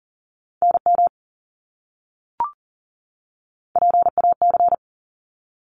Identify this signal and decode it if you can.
Morse code, heard as a keyed tone, about 40 wpm, 700 Hz. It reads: NM PAC
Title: ARRL Sections 40wpm